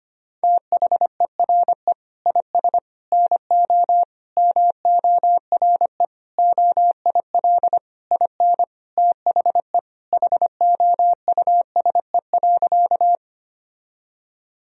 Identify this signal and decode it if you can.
Morse code, heard as a keyed tone, about 25 wpm, 700 Hz. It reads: THERE IS NO MORE OIL IN THE HOUSE.
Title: Sentences from Top 1000 Words - Review Part 4 - 25wpm